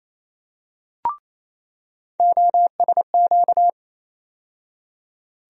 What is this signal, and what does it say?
Morse code, heard as a keyed tone, about 28 wpm, 700 Hz. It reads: OSQ
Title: USA Call Sign Postfixes - 3 Letters 28wpm